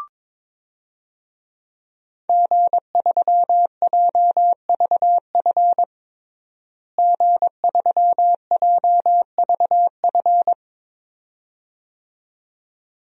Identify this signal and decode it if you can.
Morse code, heard as a keyed tone, about 22 wpm, 700 Hz. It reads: G3JVF G3JVF
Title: DXCC-100 Call Signs - 5 Characters - Repeated 2x - 22wpm